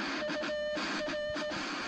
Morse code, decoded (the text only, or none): UR